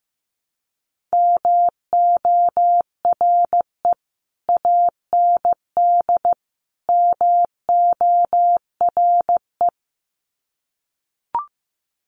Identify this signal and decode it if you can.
Morse code, heard as a keyed tone, about 15 wpm, 700 Hz. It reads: MORE AND MORE
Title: Sets of 3 Words 15wpm